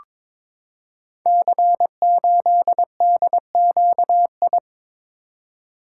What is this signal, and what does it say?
Morse code, heard as a keyed tone, about 22 wpm, 700 Hz. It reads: C8DQI
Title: Code Groups - 5 Letter:Numbers - 22wpm